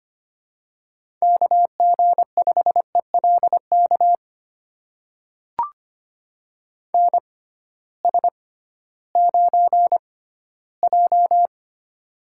Morse code, heard as a keyed tone, about 25 wpm, spaced as for 14 wpm, 700 Hz. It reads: KG5ELK NS9J
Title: USA Call Signs - 6x Character Spacing - 25wpm